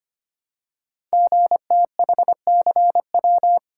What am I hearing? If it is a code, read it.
Morse code, heard as a keyed tone, about 25 wpm, 700 Hz. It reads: GTHCW
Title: Code Groups - 5 Letters - 25wpm